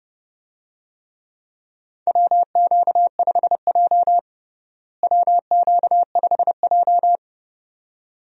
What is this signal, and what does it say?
Morse code, heard as a keyed tone, about 32 wpm, 700 Hz. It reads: WQ5J WQ5J
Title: USA Call Signs - 4 Characters - Repeated 2x - 30wpm